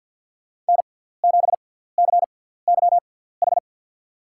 Morse code, heard as a keyed tone, about 52 wpm, spaced as for 17 wpm, 700 Hz. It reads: N7CYH
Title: USA Call Signs - 5 Characters - 6x Character Spacing - 50wpm